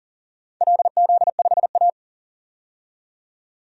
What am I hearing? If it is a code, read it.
Morse code, heard as a keyed tone, about 40 wpm, 700 Hz. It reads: LZ5A
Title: DXCC-100 Call Signs - 4 Characters - 40wpm